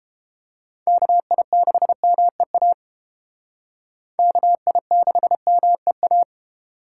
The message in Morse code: KI6MEA KI6MEA

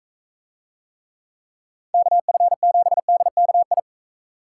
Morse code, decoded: KF7DKI

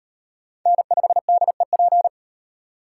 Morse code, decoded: N5DEP

